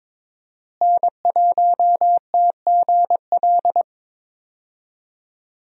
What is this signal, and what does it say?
Morse code, heard as a keyed tone, about 22 wpm, 700 Hz. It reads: N1TGL
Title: USA Call Signs - 5 Characters - Repeated 2x - 22wpm